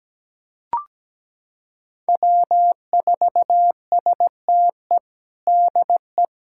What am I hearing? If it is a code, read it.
Morse code, heard as a keyed tone, about 17 wpm, 700 Hz. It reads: W4STE DE
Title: USA Callsign DE Callsign - 5 Characters - 17wpm